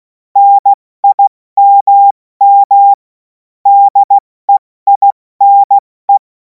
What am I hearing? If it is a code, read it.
Morse code, heard as a keyed tone, about 16 wpm, 800 Hz. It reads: NIMM DEINE
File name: Morsekurs / Witze / Witz 03. März 2026 70 F80